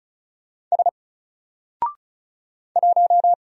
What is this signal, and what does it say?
Morse code, heard as a keyed tone, about 36 wpm, 700 Hz. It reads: S 1
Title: Single Letter-Number Alt 35wpm